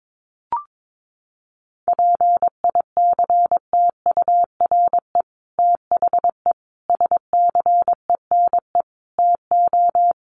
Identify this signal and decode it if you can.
Morse code, heard as a keyed tone, about 22 wpm, 700 Hz. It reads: PICTURE THE SCENE TO
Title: Sentences from Top 1100 Words 22wpm